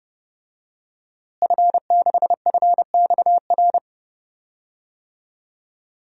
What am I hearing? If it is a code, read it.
Morse code, heard as a keyed tone, about 30 wpm, 700 Hz. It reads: F6FXR
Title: DXCC-100 Call Signs - 5 Characters - Repeated 2x - 30wpm